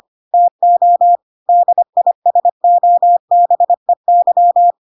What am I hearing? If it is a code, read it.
Morse code, heard as a keyed tone, about 25 wpm, 700 Hz. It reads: TO DISOBEY
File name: Pinocchio / Pinocchio ch5 25 25 ik1hge